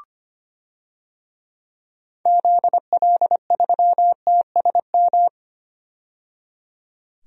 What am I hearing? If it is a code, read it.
Morse code, heard as a keyed tone, about 25 wpm, 700 Hz. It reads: ZL3TSM
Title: DXCC-100 Call Signs - 6 Characters - 25wpm